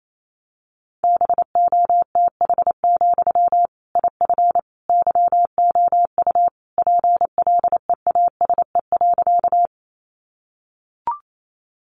Morse code, heard as a keyed tone, about 28 wpm, 700 Hz. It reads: BOTH, IF YOU PLEASE.
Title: Sentences from Top 1500 Words 28wpm